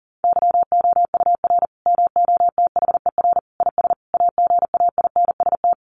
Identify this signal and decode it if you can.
Morse code, heard as a keyed tone, about 40 wpm, 700 Hz. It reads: YOUR MOTHER IS AGAINST
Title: Sentences from Top 1400 Words 40wpm